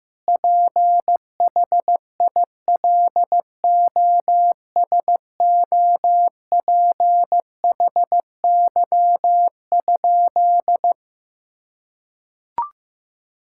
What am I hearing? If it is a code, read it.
Morse code, heard as a keyed tone, about 15 wpm, 700 Hz. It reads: PHILOSOPHY?
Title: Sentences from Top 1000 Words - Review Part 6 - 15wpm